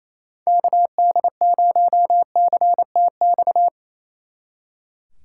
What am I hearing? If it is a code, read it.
Morse code, heard as a keyed tone, about 34 wpm, 700 Hz. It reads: KD0CTX
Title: Speed Racing - USA 2x3 Call Signs 28wpm